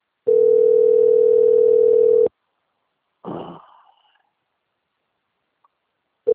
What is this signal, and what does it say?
Ringback tone (North America), 440 + 480 Hz.
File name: out- - - - - .64664